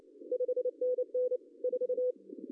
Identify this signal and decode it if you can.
Morse code, heard as a keyed tone, about 28 wpm, 500 Hz. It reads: 5NN 4
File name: CW / cqww-cw-k5tr